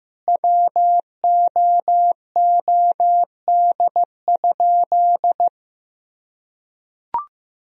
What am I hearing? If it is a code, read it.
Morse code, heard as a keyed tone, about 16 wpm, 700 Hz. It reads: WOOD?
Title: Sentences from Top 1000 Words - Review Part 2 - 15wpm